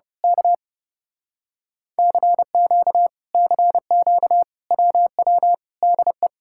Morse code, heard as a keyed tone, about 28 wpm, 700 Hz. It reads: K CQ CQ WW DE
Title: Speed-Racing Common QSO Phrases - 30wpm